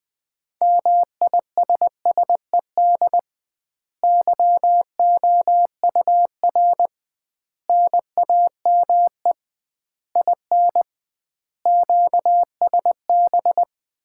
Morse code, heard as a keyed tone, about 21 wpm, 700 Hz. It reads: MISSED YOUR NAME IN QSB
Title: Common QSO Phrases - 2x Word-Spacing - 20wpm